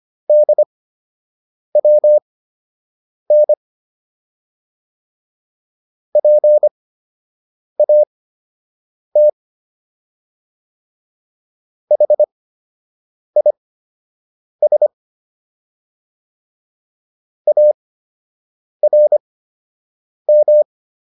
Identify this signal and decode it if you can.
Morse code, heard as a keyed tone, about 25 wpm, spaced as for 7 wpm, 600 Hz. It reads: DWN PAT HIS ARM